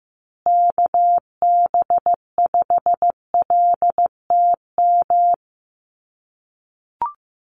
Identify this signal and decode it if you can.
Morse code, heard as a keyed tone, about 15 wpm, 700 Hz. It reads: KB5LTM
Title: USA Call Signs Repeated 2x 15wpm